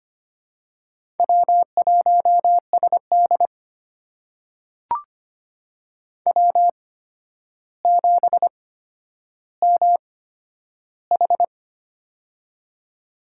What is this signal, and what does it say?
Morse code, heard as a keyed tone, about 25 wpm, spaced as for 11 wpm, 700 Hz. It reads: W1SD W7MH
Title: USA Call Signs - 4 Characters - 8x Character Spacing - 25wpm